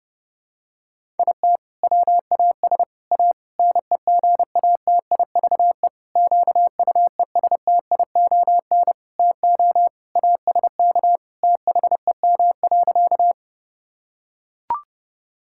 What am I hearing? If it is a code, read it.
Morse code, heard as a keyed tone, about 30 wpm, 700 Hz. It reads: IT WAS A NEGATIVE QUESTION TO ASK THEM.
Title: Sentences from Top 1000 Words - Review Part 6 - 30wpm